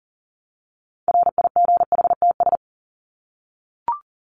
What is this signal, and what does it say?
Morse code, heard as a keyed tone, about 40 wpm, 700 Hz. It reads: RIGHTS
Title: ES:S Suffix - Top 100 Words with Base Words 40wpm